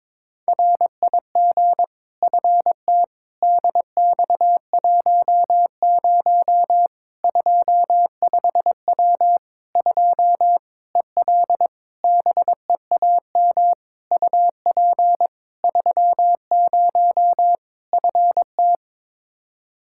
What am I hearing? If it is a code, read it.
Morse code, heard as a keyed tone, about 22 wpm, 700 Hz. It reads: RIG FT DX10 25W 2 EL BEAM UP 30 FT
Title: Practice QSO - Exchange 2 - Info - 22wpm